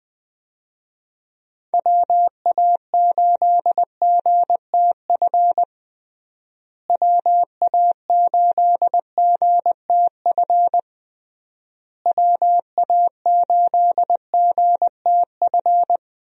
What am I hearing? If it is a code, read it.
Morse code, heard as a keyed tone, about 20 wpm, 700 Hz. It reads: WA8GTF WA8GTF WA8GTF
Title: USA Call Signs - 6 Characters - Repeated 3x - 20wpm